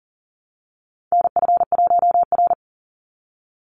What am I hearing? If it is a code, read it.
Morse code, heard as a keyed tone, about 40 wpm, 700 Hz. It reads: NF1R
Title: NA SOTA Chaser Callsigns 40wpm